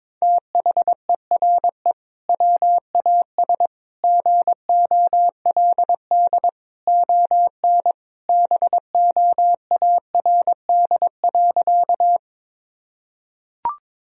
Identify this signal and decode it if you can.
Morse code, heard as a keyed tone, about 22 wpm, 700 Hz. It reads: THERE WAS GOLD ON BOARD.
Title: Sentences from Top 1000 Words - Review Part 4 - 22wpm